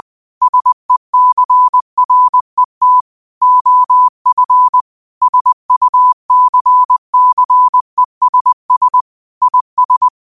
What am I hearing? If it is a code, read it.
Morse code, heard as a keyed tone, about 20 wpm, 1000 Hz. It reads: SECRET OF SUCCESS IS
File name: qotdfiles / 20wpm